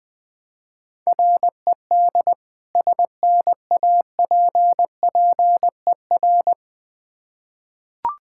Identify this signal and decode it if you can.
Morse code, heard as a keyed tone, about 21 wpm, 700 Hz. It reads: RED SNAPPER
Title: Difficult Words with Hints - 20wpm